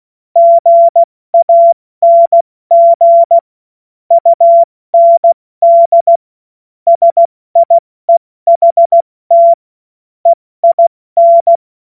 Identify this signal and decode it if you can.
Morse code, heard as a keyed tone, about 16 wpm, 680 Hz. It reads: GANG UND SIEHT EIN
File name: Morsekurs / Witze / Witz 14. Juni 2022 70 F80